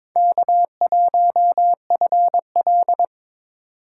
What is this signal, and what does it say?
Morse code, heard as a keyed tone, about 22 wpm, 700 Hz. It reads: K1FL
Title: USA Call Signs - 2x Character Spacing - 22wpm